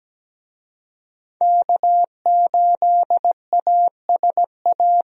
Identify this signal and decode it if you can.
Morse code, heard as a keyed tone, about 17 wpm, 700 Hz. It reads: K8ASA
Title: USA Call Signs - 5 Characters - Repeated 3x - 17wpm